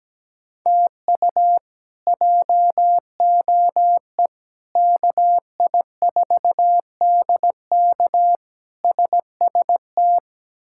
Morse code, heard as a keyed tone, about 17 wpm, 700 Hz. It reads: TU JOE KI4DK SST
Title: K1USN SST QSO Practice - 17wpm